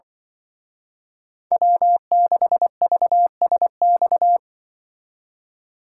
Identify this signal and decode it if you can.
Morse code, heard as a keyed tone, about 24 wpm, 700 Hz. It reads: W6VSX
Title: Speed Racing - USA 1x3 Call Signs 20wpm